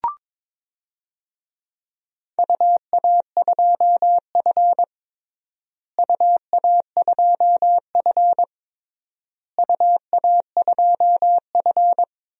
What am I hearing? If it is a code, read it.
Morse code, heard as a keyed tone, about 22 wpm, 700 Hz. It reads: UA2F UA2F UA2F
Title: DXCC-100 Call Signs - 4 Characters - Repeated 3x - 22wpm